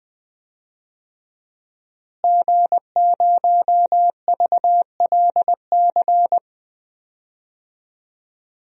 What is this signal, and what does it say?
Morse code, heard as a keyed tone, about 20 wpm, 700 Hz. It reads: G0VLC
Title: DXCC-100 Call Signs - 5 Characters - Repeated 3x - 20wpm